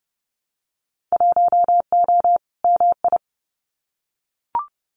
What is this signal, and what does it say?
Morse code, heard as a keyed tone, about 30 wpm, 700 Hz. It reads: 1O MI
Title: Winter Field Day Exchanges Encore 30wpm